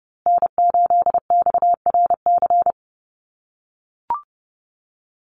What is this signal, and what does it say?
Morse code, heard as a keyed tone, about 30 wpm, 700 Hz. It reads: N8XRC
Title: USA Call Signs Repeated 3x 30wpm